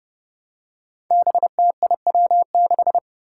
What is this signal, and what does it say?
Morse code, heard as a keyed tone, about 30 wpm, 700 Hz. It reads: BTIW6
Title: Five Challenging Letter-Number Combinations 30wpm